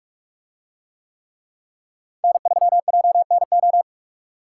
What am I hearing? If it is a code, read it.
Morse code, heard as a keyed tone, about 44 wpm, 700 Hz. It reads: N3JNO